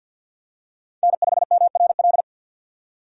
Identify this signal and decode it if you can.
Morse code, heard as a keyed tone, about 47 wpm, 700 Hz. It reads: N5MRL
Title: USA Call Signs - 5 Characters - 50wpm